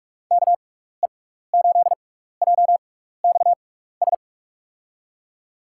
Morse code, heard as a keyed tone, about 44 wpm, spaced as for 15 wpm, 700 Hz. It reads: KE8JXS
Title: USA Call Signs - 6x Character Spacing - 45wpm